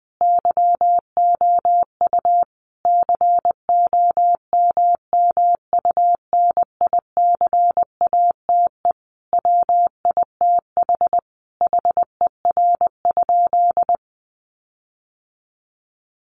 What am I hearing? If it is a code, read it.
Morse code, heard as a keyed tone, about 20 wpm, 700 Hz. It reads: YOU COMMUNICATE WITH HER?
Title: Sentences from Top 1300 Words 20wpm